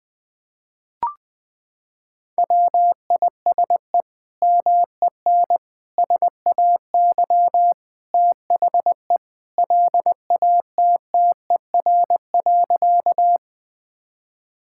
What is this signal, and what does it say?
Morse code, heard as a keyed tone, about 20 wpm, 700 Hz. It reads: WISE MEN SAY THE LATTER.
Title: Sentences from Top 1600 Words 20wpm